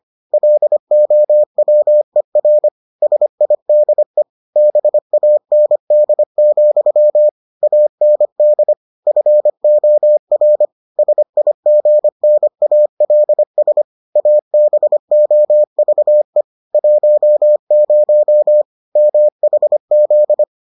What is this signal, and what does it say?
Morse code, heard as a keyed tone, about 25 wpm, 600 Hz. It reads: LOWER SIDE BAND, AND FOR SIGNALS ABOVE 10 MHZ